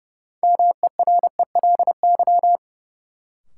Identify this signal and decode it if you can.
Morse code, heard as a keyed tone, about 30 wpm, 700 Hz. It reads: MERELY
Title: Sentences from Top 1400 Words - New Word Drill Alt - 30wpm